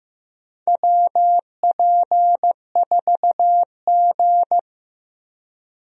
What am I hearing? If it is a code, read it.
Morse code, heard as a keyed tone, about 15 wpm, 700 Hz. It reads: WP4G
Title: USA Call Signs - 4 Characters - 15wpm